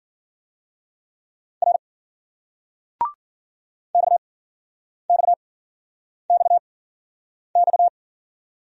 Morse code, heard as a keyed tone, about 53 wpm, spaced as for 9 wpm, 700 Hz. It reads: R XXXX